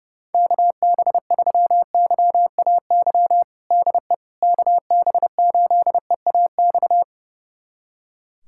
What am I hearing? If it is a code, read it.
Morse code, heard as a keyed tone, about 32 wpm, 700 Hz. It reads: KB3YAY DE KB8EAX
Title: USA Callsign DE Callsign - 6 Characters - 30wpm